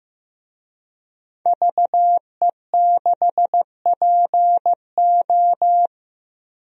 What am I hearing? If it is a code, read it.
Morse code, heard as a keyed tone, about 15 wpm, 700 Hz. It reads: VE6PO
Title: DXCC-100 Call Signs - 5 Characters - Repeated 3x - 15wpm